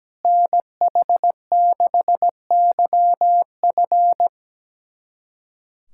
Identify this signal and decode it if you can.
Morse code, heard as a keyed tone, about 17 wpm, 700 Hz. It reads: NH6YF
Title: USA Callsign DE Callsign - 5 Characters - 17wpm